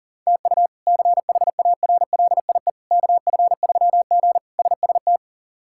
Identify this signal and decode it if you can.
Morse code, heard as a keyed tone, about 40 wpm, 700 Hz. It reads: TU CHARLIE KF3G SST
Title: K1USN SST QSO Practice - 40wpm